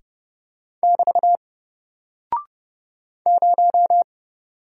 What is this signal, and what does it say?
Morse code, heard as a keyed tone, about 30 wpm, 700 Hz. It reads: <BT> 0
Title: Single Character (Letter:Number:Punctuation:Prosign) Alt 30wpm